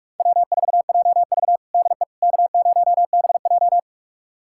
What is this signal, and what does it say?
Morse code, heard as a keyed tone, about 42 wpm, 700 Hz. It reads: W4JV DE K0BJ
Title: USA Callsign DE Callsign - 4 Characters - 45wpm